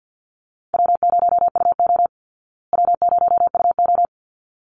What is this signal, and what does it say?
Morse code, heard as a keyed tone, about 48 wpm, 700 Hz. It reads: W0UO W0UO
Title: Easy Call Signs Repeated 2x - 50wpm